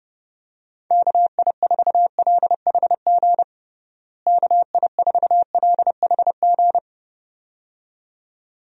Morse code, heard as a keyed tone, about 30 wpm, 700 Hz. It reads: KI4LHG KI4LHG
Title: USA Call Signs - 6 Characters - Repeated 2x - 30wpm